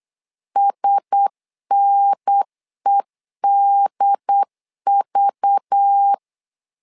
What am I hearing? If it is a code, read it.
Morse code, heard as a keyed tone, about 8 wpm, 800 Hz. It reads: SNEDV
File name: LEZIONI / 5 LBVC / 5 LBV 8wpm 2837